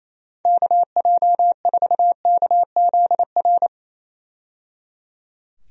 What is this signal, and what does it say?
Morse code, heard as a keyed tone, about 28 wpm, 700 Hz. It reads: KJ4KZR
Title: USA Call Signs - 6 Characters - 28wpm